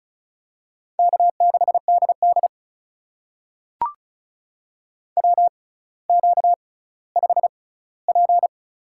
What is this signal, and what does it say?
Morse code, heard as a keyed tone, about 36 wpm, spaced as for 18 wpm, 700 Hz. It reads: K6DD WQ5P
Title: USA Call Signs - 4 Characters - 6x Character Spacing - 35wpm